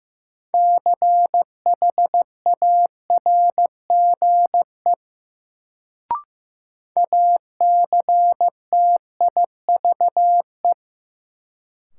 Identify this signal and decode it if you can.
Morse code, heard as a keyed tone, about 15 wpm, 700 Hz. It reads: CHARGE ACTIVE
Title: Sentences from Top 900 Words - New Word Drill - 15wpm